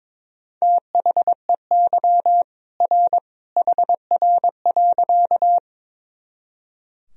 Morse code, heard as a keyed tone, about 22 wpm, 700 Hz. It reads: THEY R HR.
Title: Sentences from Top 200 Words and Abbreviations 22wpm